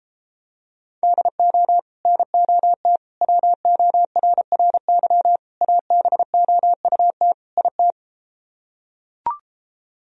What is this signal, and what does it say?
Morse code, heard as a keyed tone, about 33 wpm, 700 Hz. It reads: DO NOT WORRY ABOUT IT